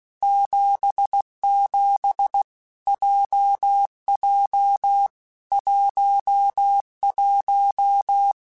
Morse code, heard as a keyed tone, about 16 wpm, 800 Hz. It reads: 77 JJ 11
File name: Basic Letter Groups By7Groups / JZ17 / JZ17@15